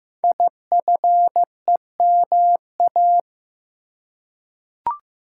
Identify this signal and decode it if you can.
Morse code, heard as a keyed tone, about 15 wpm, 700 Hz. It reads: IFEMA
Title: Code Groups - 5 Letter:Numbers - 15wpm